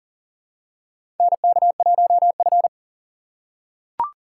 Morse code, heard as a keyed tone, about 40 wpm, 700 Hz. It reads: NK1F
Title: USA Call Signs - 3x Character Spacing - 40wpm